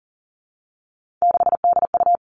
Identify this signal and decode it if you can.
Morse code, heard as a keyed tone, about 38 wpm, 700 Hz. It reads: 6DU
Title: Three Challenging Letter-Number Combinations 40wpm